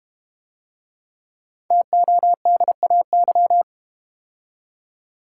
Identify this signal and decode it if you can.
Morse code, heard as a keyed tone, about 32 wpm, 700 Hz. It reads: TODAY